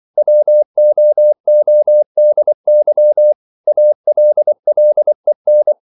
Morse code, heard as a keyed tone, about 24 wpm, 600 Hz. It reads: WOODY ALLEN
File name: CW / Practice / EnglishQuotations0324